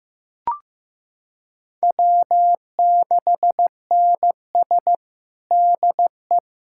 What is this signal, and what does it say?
Morse code, heard as a keyed tone, about 15 wpm, 700 Hz. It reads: W6NS DE
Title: Practice QSO - Exchange 4 - Ending - 15wpm